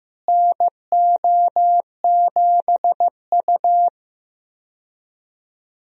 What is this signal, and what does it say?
Morse code, heard as a keyed tone, about 15 wpm, 700 Hz. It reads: NO7U
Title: USA Call Signs - 4 Characters - Repeated 2x - 15wpm